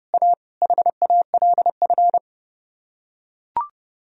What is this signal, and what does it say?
Morse code, heard as a keyed tone, about 28 wpm, 700 Hz. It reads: A HALF